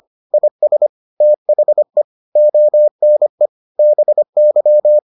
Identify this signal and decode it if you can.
Morse code, heard as a keyed tone, about 25 wpm, 600 Hz. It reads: IS THE ONE BY